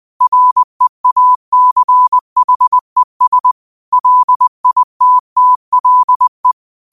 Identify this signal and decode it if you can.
Morse code, heard as a keyed tone, about 20 wpm, 1000 Hz. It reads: REACHES LITTLE
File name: qotdfiles / 20wpm